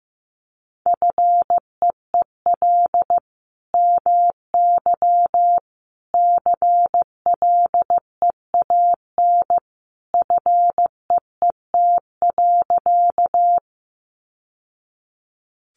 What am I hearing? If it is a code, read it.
Morse code, heard as a keyed tone, about 15 wpm, 700 Hz. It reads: FEEL MY CLEAN FEET.
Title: Sentences from Top 1100 Words 15wpm